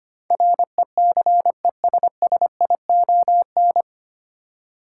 Morse code, heard as a keyed tone, about 25 wpm, 700 Hz. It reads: RECESSION
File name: practice / 25wpm / 9 Letter Words - First 3 Letters Given - 25wpm